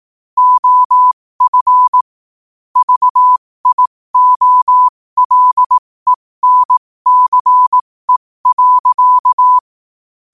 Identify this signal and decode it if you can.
Morse code, heard as a keyed tone, about 18 wpm, spaced as for 15 wpm, 1000 Hz. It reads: OF VIOLENCE.